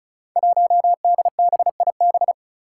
Morse code, heard as a keyed tone, about 34 wpm, 700 Hz. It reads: 1DBIB
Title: Five Challenging Letter-Number Combinations 35wpm